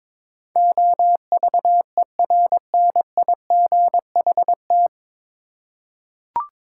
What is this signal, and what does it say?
Morse code, heard as a keyed tone, about 22 wpm, 700 Hz. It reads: OVERNIGHT
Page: Morse Code Ninja practice 22wpm